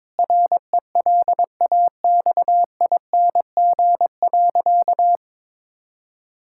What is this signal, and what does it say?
Morse code, heard as a keyed tone, about 22 wpm, 700 Hz. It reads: RELAXING.
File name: Difficult Sentences with Hints - 22wpm